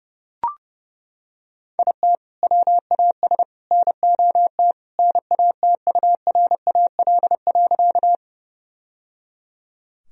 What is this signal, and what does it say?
Morse code, heard as a keyed tone, about 30 wpm, 700 Hz. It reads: IT WAS NOT NATURAL.
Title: Sentences from Top 1000 Words - Review Part 4 - 30wpm